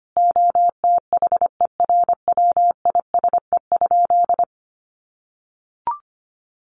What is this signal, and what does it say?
Morse code, heard as a keyed tone, about 25 wpm, 700 Hz. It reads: OTHERWISE?
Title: Sentences from Top 1200 Words 25wpm